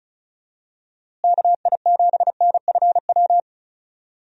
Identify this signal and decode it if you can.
Morse code, heard as a keyed tone, about 34 wpm, 700 Hz. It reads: KI7NFW
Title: USA Call Signs - 6 Characters - Repeated 2x - 35wpm